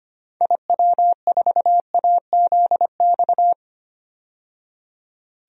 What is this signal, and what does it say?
Morse code, heard as a keyed tone, about 25 wpm, 700 Hz. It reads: IW4AZX
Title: DXCC-100 Call Signs - 6 Characters - 25wpm